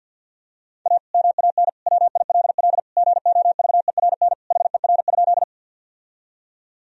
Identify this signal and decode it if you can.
Morse code, heard as a keyed tone, about 46 wpm, 700 Hz. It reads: A MAN WILL GOVERN HER?
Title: Sentences from Top 1000 Words - Review Part 2 - 50wpm